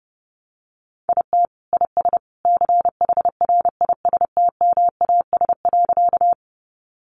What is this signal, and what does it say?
Morse code, heard as a keyed tone, about 30 wpm, 700 Hz. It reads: IT IS CHRISTMAS.
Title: Easy Phrases 30wpm